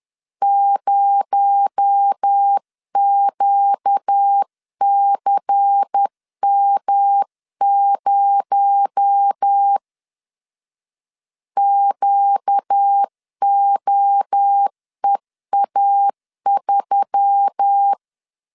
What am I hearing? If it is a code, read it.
Morse code, heard as a keyed tone, about 10 wpm, 800 Hz. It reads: 0QCM0 QOEA3
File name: CW / DET / 1 6 1500C S C 10wpm 4252